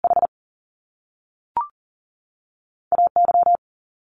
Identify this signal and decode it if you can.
Morse code, heard as a keyed tone, about 38 wpm, spaced as for 3 wpm, 700 Hz. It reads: HA Y
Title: Top Two Letter Combinations in Words 40wpm